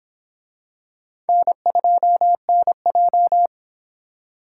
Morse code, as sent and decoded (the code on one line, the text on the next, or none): -. ..--- -. .---
N2NJ